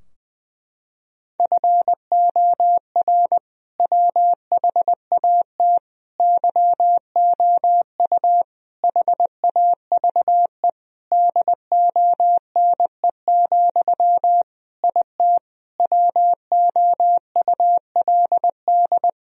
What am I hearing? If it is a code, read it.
Morse code, heard as a keyed tone, about 21 wpm, 700 Hz. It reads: FOR WHAT YOU HAVE DONE, IT WOULD
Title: Sentences from Top 1000 Words - Review Part 2 - 20wpm